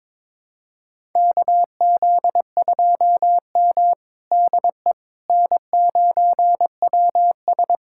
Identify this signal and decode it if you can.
Morse code, heard as a keyed tone, about 22 wpm, 700 Hz. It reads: KZ2M DE N9WS
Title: USA Callsign DE Callsign - 4 Characters - 22wpm